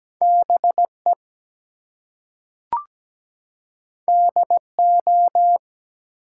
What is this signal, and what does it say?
Morse code, heard as a keyed tone, about 17 wpm, 700 Hz. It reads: BE DO
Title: Two Letter Words - 17wpm